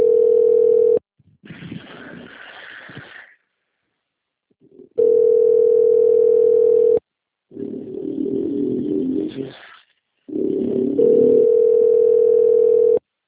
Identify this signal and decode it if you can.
Ringback tone (North America), 440 + 480 Hz.